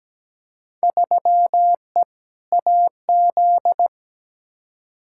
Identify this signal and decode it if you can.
Morse code, heard as a keyed tone, about 17 wpm, 700 Hz. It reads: 3E AZ
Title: Field Day Exchanges - 17wpm